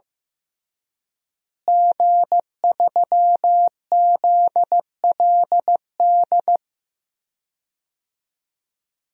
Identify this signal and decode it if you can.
Morse code, heard as a keyed tone, about 15 wpm, 700 Hz. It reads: G3ZLD
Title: DXCC-100 Call Signs - 5 Characters - Repeated 3x - 15wpm